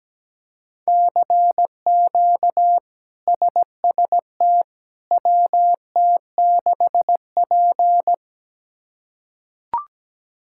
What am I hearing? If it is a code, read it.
Morse code, heard as a keyed tone, about 17 wpm, 700 Hz. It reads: CQ SST WT6P
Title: K1USN SST QSO Practice - 17wpm